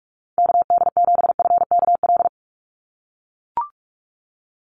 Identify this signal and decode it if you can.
Morse code, heard as a keyed tone, about 44 wpm, 700 Hz. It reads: KD7FKL
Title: USA Call Signs 45wpm